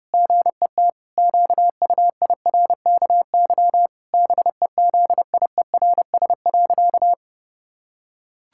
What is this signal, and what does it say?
Morse code, heard as a keyed tone, about 30 wpm, 700 Hz. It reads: GET QUIRKY BEZIERS.
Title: Pangram Challenge v2 - 30wpm